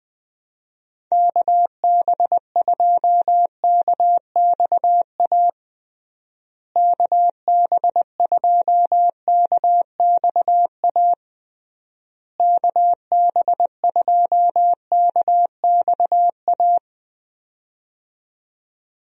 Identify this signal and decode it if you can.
Morse code, heard as a keyed tone, about 20 wpm, 700 Hz. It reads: KB2KXA KB2KXA KB2KXA
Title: USA Call Signs - 6 Characters - Repeated 3x - 20wpm